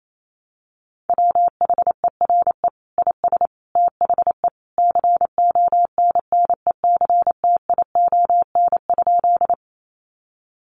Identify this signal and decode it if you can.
Morse code, heard as a keyed tone, about 28 wpm, 700 Hz. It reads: WHERE IS THE CONNECTION?
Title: Sentences from Top 1300 Words 28wpm